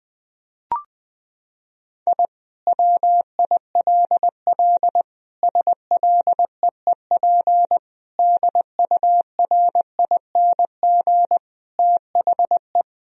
Morse code, heard as a keyed tone, about 21 wpm, 700 Hz. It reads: I WILL SLEEP DURING THE
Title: Sentences from Top 1000 Words - Review Part 2 - 20wpm